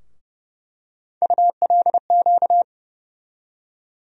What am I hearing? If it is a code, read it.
Morse code, heard as a keyed tone, about 30 wpm, 700 Hz. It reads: ULQ
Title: USA Call Sign Postfixes - 3 Letters 30wpm